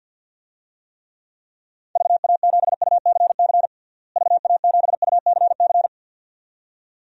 Morse code, heard as a keyed tone, about 47 wpm, 700 Hz. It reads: VA7UCC VA7UCC
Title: DXCC-100 Call Signs - 6 Characters - Repeated 2x - 50wpm